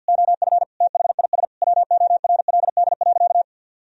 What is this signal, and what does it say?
Morse code, heard as a keyed tone, about 50 wpm, 700 Hz. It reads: OF THIS WORLD.
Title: Sentences from Top 1000 Words - Review Part 1 - 50wpm